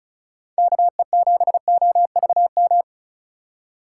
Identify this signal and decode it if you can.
Morse code, heard as a keyed tone, about 36 wpm, 700 Hz. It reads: KE7OVM
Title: USA Call Signs - 6 Characters - 3x Character Spacing - 35wpm